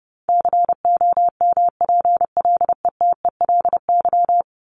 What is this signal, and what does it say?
Morse code, heard as a keyed tone, about 30 wpm, 700 Hz. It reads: COMPLETELY
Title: LY Suffix - Top 100 Words with Base Words 30wpm